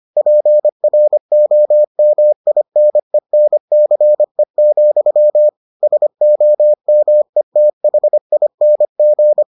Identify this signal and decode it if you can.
Morse code, heard as a keyed tone, about 25 wpm, 600 Hz. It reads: PROMINENCE, SOMETHING